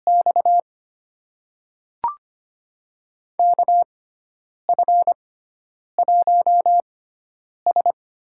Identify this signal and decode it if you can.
Morse code, heard as a keyed tone, about 25 wpm, spaced as for 9 wpm, 700 Hz. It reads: X KF1S
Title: USA Call Signs - 4 Characters - 6x Character Spacing - 25wpm